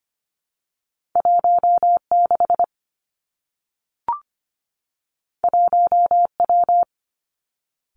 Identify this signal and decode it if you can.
Morse code, heard as a keyed tone, about 25 wpm, 700 Hz. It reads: 1 6 1 W
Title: Two Challenging Letter-Number Combinations 25wpm